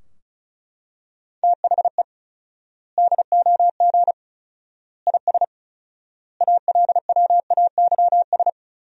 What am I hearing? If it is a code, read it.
Morse code, heard as a keyed tone, about 36 wpm, 700 Hz. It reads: THE DOG IS ALWAYS
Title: 4x Word-Spacing - Sentences from Top 400 Words - 35wpm